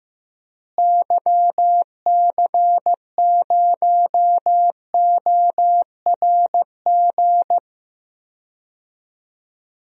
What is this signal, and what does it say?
Morse code, heard as a keyed tone, about 15 wpm, 700 Hz. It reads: YC0ORG
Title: DXCC-100 Call Signs - 6 Characters - Repeated 2x - 15wpm